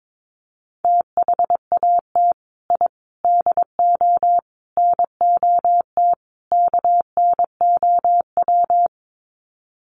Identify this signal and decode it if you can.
Morse code, heard as a keyed tone, about 22 wpm, 700 Hz. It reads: THAT I DO NOT KNOW
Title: Sets of 5 Words Encore2 22wpm